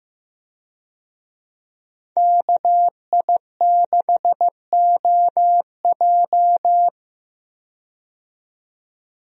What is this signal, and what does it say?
Morse code, heard as a keyed tone, about 15 wpm, 700 Hz. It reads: KI6OJ
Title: USA Call Signs - 5 Characters - 15wpm